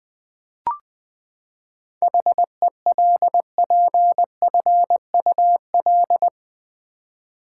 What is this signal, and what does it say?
Morse code, heard as a keyed tone, about 20 wpm, 700 Hz. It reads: HELPFUL
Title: Seven Letter Words - 20wpm